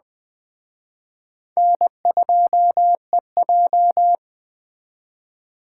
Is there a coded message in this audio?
Yes: Morse code from 1.6 to 4.1 s, heard as a keyed tone, about 20 wpm, 700 Hz. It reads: N2EJ